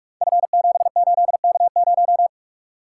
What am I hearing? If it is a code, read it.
Morse code, heard as a keyed tone, about 44 wpm, 690 Hz. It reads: F78K0
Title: Code Groups - 5 Letter:Numbers - 45wpm